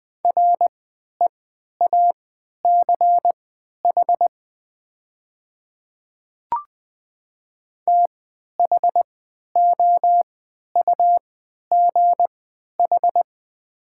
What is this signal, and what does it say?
Morse code, heard as a keyed tone, about 20 wpm, spaced as for 11 wpm, 700 Hz. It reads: REACH THOUGH
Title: 3x Character Spacing - Top 500 Words - 20wpm